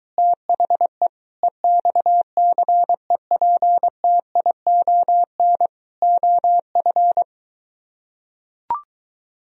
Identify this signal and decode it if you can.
Morse code, heard as a keyed tone, about 23 wpm, 700 Hz. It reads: THE EXCEPTION OF